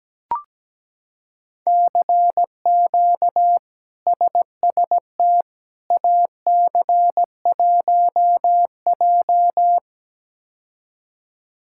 Morse code, heard as a keyed tone, about 17 wpm, 700 Hz. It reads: CQ SST AC1J
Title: K1USN SST QSO Practice - 17wpm